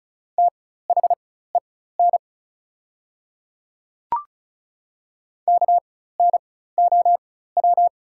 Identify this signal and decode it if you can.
Morse code, heard as a keyed tone, about 36 wpm, spaced as for 16 wpm, 690 Hz. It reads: THEN KNOW
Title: 4x Character Spacing - Top 200 Words - 35wpm